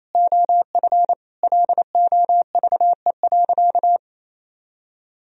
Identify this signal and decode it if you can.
Morse code, heard as a keyed tone, about 28 wpm, 700 Hz. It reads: OF LOVE.
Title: Sentences from Top 1000 Words - Review Part 3 - 28wpm